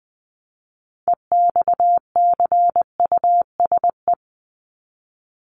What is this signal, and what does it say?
Morse code, heard as a keyed tone, about 20 wpm, 700 Hz. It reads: EXCUSE
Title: Top 501-750 Words 20wpm